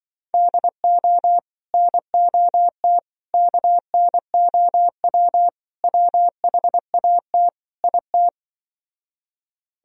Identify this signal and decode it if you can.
Morse code, heard as a keyed tone, about 25 wpm, 700 Hz. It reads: DO NOT KNOW WHAT IT
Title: Speed-Racing - Sets of 5 Words 20wpm